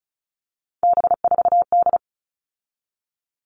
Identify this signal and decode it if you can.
Morse code, heard as a keyed tone, about 34 wpm, 700 Hz. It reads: B4D
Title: All Cut Numbers - 3-Digit 35wpm